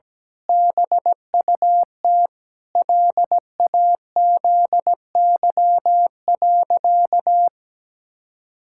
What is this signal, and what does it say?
Morse code, heard as a keyed tone, about 17 wpm, 700 Hz. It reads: BUT LAZY.
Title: Pangram Challenge v2 - 17wpm